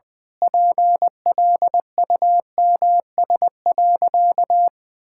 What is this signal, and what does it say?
Morse code, heard as a keyed tone, about 21 wpm, 700 Hz. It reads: PLUMS.
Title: Pangram Challenge v2 - 20wpm